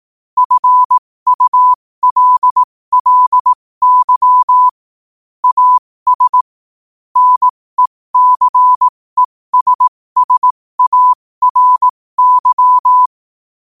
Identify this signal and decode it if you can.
Morse code, heard as a keyed tone, about 18 wpm, spaced as for 15 wpm, 1000 Hz. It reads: FULLY AS NECESSARY
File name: qotdfiles / 15wpm